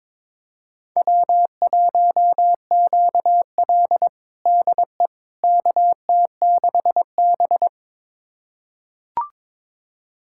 Morse code, heard as a keyed tone, about 22 wpm, 700 Hz. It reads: W1QL DE KT6B
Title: USA Callsign DE Callsign - 4 Characters - 22wpm